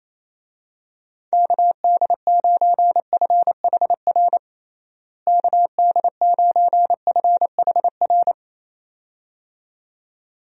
Morse code, heard as a keyed tone, about 28 wpm, 700 Hz. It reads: KD9FHR KD9FHR
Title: USA Call Signs - 6 Characters - Repeated 2x - 28wpm